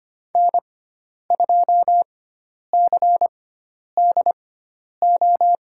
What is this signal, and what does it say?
Morse code, heard as a keyed tone, about 25 wpm, spaced as for 10 wpm, 700 Hz. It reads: N2CDO
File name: Lesson 16g - Letter c - Call Signs 25wpm